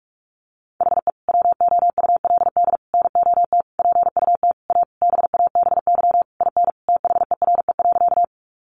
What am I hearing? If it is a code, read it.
Morse code, heard as a keyed tone, about 46 wpm, 700 Hz. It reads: HE WOULD NOT PUT A BABY IN THERE.
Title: Easy Phrases 45wpm